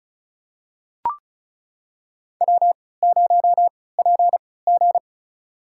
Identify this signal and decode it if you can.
Morse code, heard as a keyed tone, about 36 wpm, spaced as for 20 wpm, 700 Hz. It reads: W0PG
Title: USA Call Signs - 4 Characters - 3x Character Spacing - 35wpm